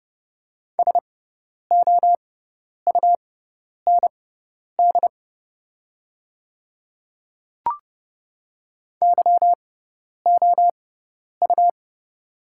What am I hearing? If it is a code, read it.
Morse code, heard as a keyed tone, about 30 wpm, spaced as for 10 wpm, 700 Hz. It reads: SOUND YOU
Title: 6x Character Spacing - Top 100 Words - 30wpm